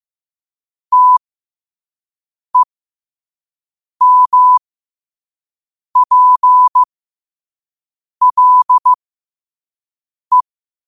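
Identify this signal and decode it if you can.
Morse code, heard as a keyed tone, about 15 wpm, spaced as for 5 wpm, 1000 Hz. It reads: TEMPLE
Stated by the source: Morse code sent at 5 words per minute.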